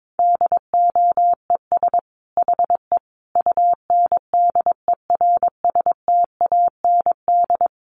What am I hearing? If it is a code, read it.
Morse code, heard as a keyed tone, about 22 wpm, 700 Hz. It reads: DOES HE UNDERSTAND
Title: Sentences from Top 1100 Words 22wpm